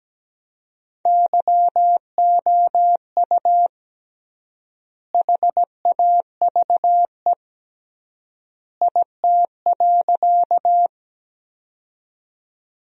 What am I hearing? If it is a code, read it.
Morse code, heard as a keyed tone, about 17 wpm, 700 Hz. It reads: YOU HAVE IT.
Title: 3x Word-Spacing - Sentences from Top 100 Words - 17wpm